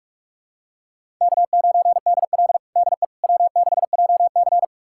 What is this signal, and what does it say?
Morse code, heard as a keyed tone, about 42 wpm, 700 Hz. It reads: K9DL DE W6JC
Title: USA Callsign DE Callsign - 4 Characters - 45wpm